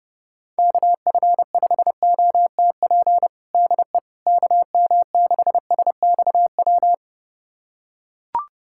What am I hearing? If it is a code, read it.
Morse code, heard as a keyed tone, about 31 wpm, 700 Hz. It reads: KF5OTP DE KM6SXW
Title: USA Callsign DE Callsign 30wpm